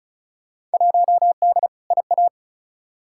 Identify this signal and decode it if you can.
Morse code, heard as a keyed tone, about 34 wpm, 700 Hz. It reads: 1D IA
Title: Field Day Exchanges - 35wpm